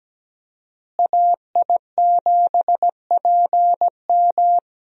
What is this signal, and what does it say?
Morse code, heard as a keyed tone, about 17 wpm, 700 Hz. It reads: AI7PM
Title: USA Call Signs - 5 Characters - Repeated 3x - 17wpm